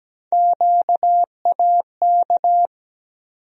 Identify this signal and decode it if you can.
Morse code, heard as a keyed tone, about 17 wpm, 700 Hz. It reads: QAK
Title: USA Call Sign Postfixes - 3 Letters 17wpm